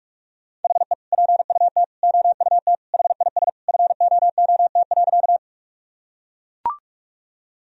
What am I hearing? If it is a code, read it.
Morse code, heard as a keyed tone, about 41 wpm, 700 Hz. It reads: HE PUT OUT HIS FOOT.
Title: Sentences from Top 1000 Words - Review Part 3 - 45wpm